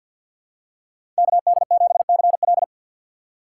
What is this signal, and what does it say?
Morse code, heard as a keyed tone, about 50 wpm, 700 Hz. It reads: KD7CL
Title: USA Call Signs - 5 Characters - 6x Character Spacing - 50wpm